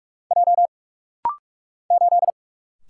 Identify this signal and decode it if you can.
Morse code, heard as a keyed tone, about 46 wpm, 700 Hz. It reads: J 8
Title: Single Letter-Number - Rapid-Fire 45wpm